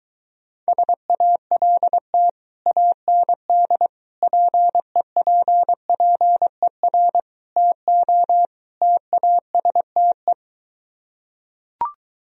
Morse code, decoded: SALT AND PEPPER TO TASTE

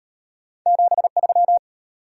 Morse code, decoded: 73

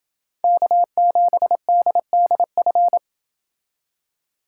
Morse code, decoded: K7DDF

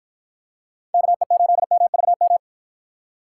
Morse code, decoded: KE8MVM

